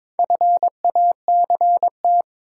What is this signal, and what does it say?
Morse code, heard as a keyed tone, about 22 wpm, 700 Hz. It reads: FACT
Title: Sentences from Top 500 Words - New Word Drill - 22wpm